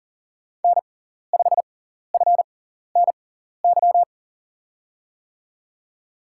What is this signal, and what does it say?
Morse code, heard as a keyed tone, about 40 wpm, spaced as for 14 wpm, 700 Hz. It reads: N5FNY
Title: USA Call Signs - 5 Characters - 6x Character Spacing - 40wpm